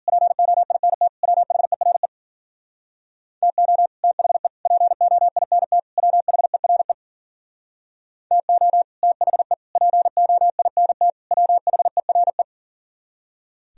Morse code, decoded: POINT WHERE TO THE POINT WHERE TO THE POINT WHERE

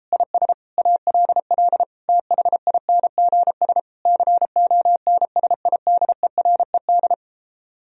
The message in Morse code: IS ALL THINGS CONSIDERED

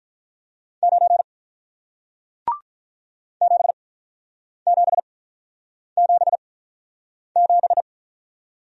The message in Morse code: 9 7 7 7 7

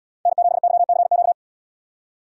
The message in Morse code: N7YCQ

